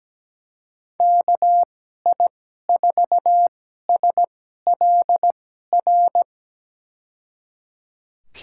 Morse code, heard as a keyed tone, about 17 wpm, spaced as for 12 wpm, 700 Hz. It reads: KI4SLR
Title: USA Call Signs - 2x Character Spacing - 17wpm